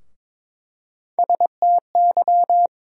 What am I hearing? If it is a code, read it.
Morse code, heard as a keyed tone, about 21 wpm, 700 Hz. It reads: STY
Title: USA Call Sign Postfixes - 3 Letters 22wpm